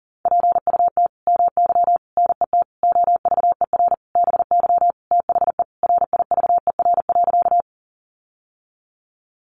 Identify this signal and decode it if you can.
Morse code, heard as a keyed tone, about 40 wpm, 700 Hz. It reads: PUT MY NET OVER BY THE RIVER.
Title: Sentences from Top 1400 Words 40wpm